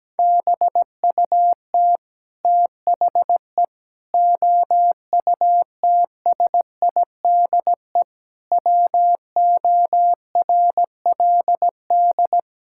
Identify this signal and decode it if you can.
Morse code, heard as a keyed tone, about 17 wpm, 700 Hz. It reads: BUT THE OUTSIDE WORLD
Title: Sentences from Top 1000 Words - Review Part 6 - 17wpm